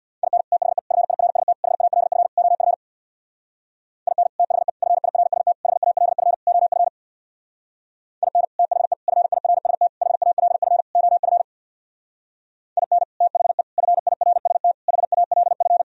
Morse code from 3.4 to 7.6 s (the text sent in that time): IN THE FIRST HALF OF